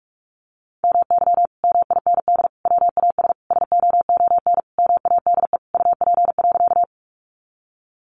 Morse code, heard as a keyed tone, about 44 wpm, 700 Hz. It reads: MY MIND WAS SOON MADE UP.
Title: Sentences from Top 500 Words 45wpm